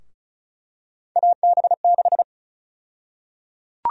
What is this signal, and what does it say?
Morse code, heard as a keyed tone, about 34 wpm, 700 Hz. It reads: AB6
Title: USA Call Sign Prefixes - 35wpm